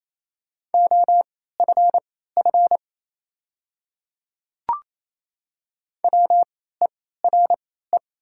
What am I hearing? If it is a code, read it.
Morse code, heard as a keyed tone, about 28 wpm, spaced as for 16 wpm, 700 Hz. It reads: OFF WERE
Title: 3x Character Spacing - Top 300 Words - 28wpm